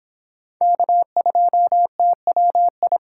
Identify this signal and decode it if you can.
Morse code, heard as a keyed tone, about 23 wpm, 700 Hz. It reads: K2TWI